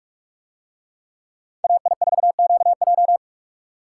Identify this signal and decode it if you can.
Morse code, heard as a keyed tone, about 46 wpm, 700 Hz. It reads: AI4QJ